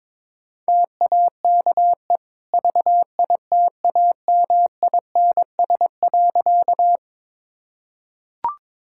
Morse code, heard as a keyed tone, about 22 wpm, 690 Hz. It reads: TAKE VITAMINS.
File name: practice / 22wpm / Easy Sentences with Hints - 22wpm